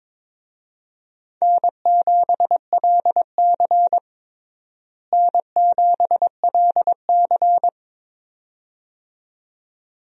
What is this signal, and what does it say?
Morse code, heard as a keyed tone, about 22 wpm, 700 Hz. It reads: N7LC N7LC
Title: USA Call Signs - 4 Characters - Repeated 2x - 22wpm